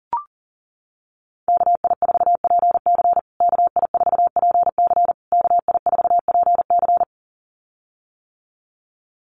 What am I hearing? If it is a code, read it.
Morse code, heard as a keyed tone, about 40 wpm, 700 Hz. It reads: KI4PC KI4PC KI4PC
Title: USA Call Signs Repeated 3x 40wpm